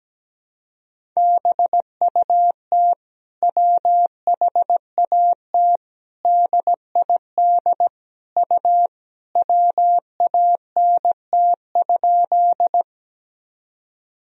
Morse code, heard as a keyed tone, about 17 wpm, 700 Hz. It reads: BUT WHAT DID U WANT?
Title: Sentences from Top 200 Words and Abbreviations 17wpm